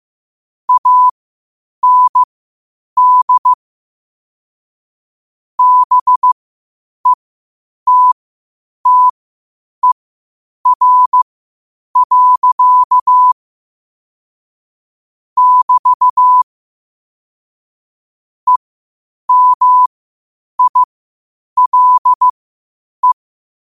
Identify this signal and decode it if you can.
Morse code, heard as a keyed tone, about 15 wpm, spaced as for 8 wpm, 1000 Hz. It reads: AND BETTER. <BT> EMILE
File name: qotdfiles / 8wpm